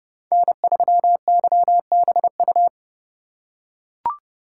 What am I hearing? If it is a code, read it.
Morse code, heard as a keyed tone, about 30 wpm, 700 Hz. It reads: N3YBU
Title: USA Call Signs - 5 Characters - 2x Character Spacing - 30wpm